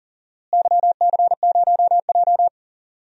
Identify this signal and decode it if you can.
Morse code, heard as a keyed tone, about 40 wpm, 700 Hz. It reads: YC0J
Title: DXCC-100 Call Signs - 4 Characters - Repeated 3x - 40wpm